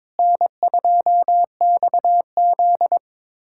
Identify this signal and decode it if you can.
Morse code, heard as a keyed tone, about 22 wpm, 700 Hz. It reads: N2XZ
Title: USA Callsign DE Callsign - 4 Characters - 22wpm